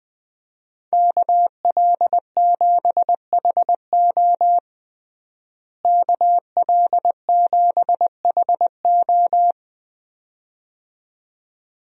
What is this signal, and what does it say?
Morse code, heard as a keyed tone, about 20 wpm, 700 Hz. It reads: KL7HO KL7HO
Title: DXCC-100 Call Signs - 5 Characters - Repeated 2x - 20wpm